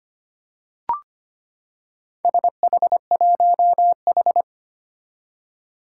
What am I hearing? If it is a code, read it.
Morse code, heard as a keyed tone, about 25 wpm, 700 Hz. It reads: SH1H
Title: Four Challenging Letter-Number Combinations 25wpm